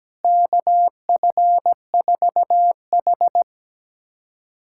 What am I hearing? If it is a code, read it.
Morse code, heard as a keyed tone, about 17 wpm, 700 Hz. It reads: KF4H
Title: USA Callsign DE Callsign - 4 Characters - 17wpm